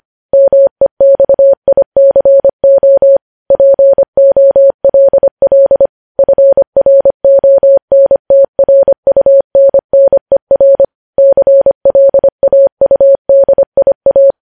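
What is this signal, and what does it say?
Morse code, heard as a keyed tone, about 25 wpm, 550 Hz. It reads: MEXICO POLL FRONTRUNNER CLAUDIA
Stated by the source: Morse code news headlines sent at 25 wpm.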